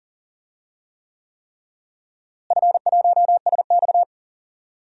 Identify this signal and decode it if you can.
Morse code, heard as a keyed tone, about 40 wpm, 700 Hz. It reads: F1SX
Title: DXCC-100 Call Signs - 4 Characters - Repeated 3x - 40wpm